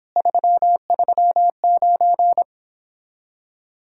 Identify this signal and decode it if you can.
Morse code, heard as a keyed tone, about 29 wpm, 700 Hz. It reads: 339